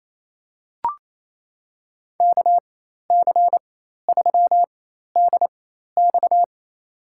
Morse code, heard as a keyed tone, about 28 wpm, spaced as for 13 wpm, 700 Hz. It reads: KC3DX
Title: USA Call Signs - 5 Characters - 4x Character Spacing - 28wpm